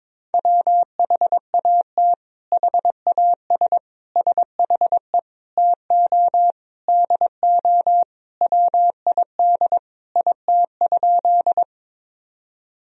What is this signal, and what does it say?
Morse code, heard as a keyed tone, about 22 wpm, 700 Hz. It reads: WHAT HAS SHE TO DO WID IT?
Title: Sentences from Top 100 Words and Abbreviations 22wpm